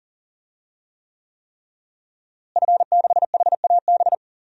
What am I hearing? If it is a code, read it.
Morse code, heard as a keyed tone, about 37 wpm, 700 Hz. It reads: F6HAB
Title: DXCC-100 Call Signs - 5 Characters - Repeated 3x - 40wpm